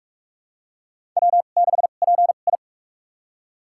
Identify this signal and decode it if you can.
Morse code, heard as a keyed tone, about 46 wpm, spaced as for 32 wpm, 700 Hz. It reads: W6PI
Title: USA Call Signs - 4 Characters - 2x Character Spacing - 45wpm